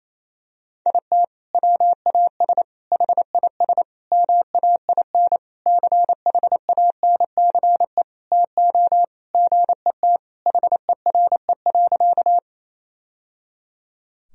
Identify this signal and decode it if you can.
Morse code, heard as a keyed tone, about 28 wpm, 700 Hz. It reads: IT WAS HIS MAIN CHANCE TO GET HERE.
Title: Sentences from Top 1000 Words - Review Part 2 - 28wpm